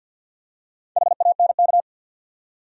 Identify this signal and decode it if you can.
Morse code, heard as a keyed tone, about 51 wpm, 700 Hz. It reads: HANK
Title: USA Common Names - Max 5 Letters - 50wpm